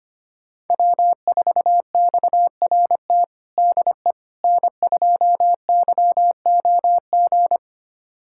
Morse code, heard as a keyed tone, about 25 wpm, 700 Hz. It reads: W4XRT DE N2YOG
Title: USA Callsign DE Callsign 25wpm